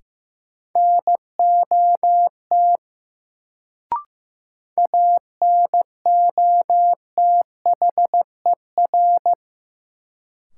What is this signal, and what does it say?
Morse code, heard as a keyed tone, about 15 wpm, 700 Hz. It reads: NOT ANOTHER
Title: Sentences from Top 600 Words - New Word Drill - 15wpm